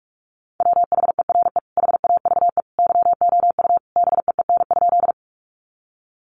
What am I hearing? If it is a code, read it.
Morse code, heard as a keyed tone, about 44 wpm, 700 Hz. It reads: WHERE HAVE YOU BEEN?
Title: Sentences from Top 200 Words 45wpm